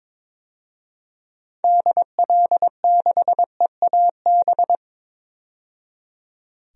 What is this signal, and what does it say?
Morse code, heard as a keyed tone, about 22 wpm, 700 Hz. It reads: DL6EAB
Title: DXCC-100 Call Signs - 6 Characters - 22wpm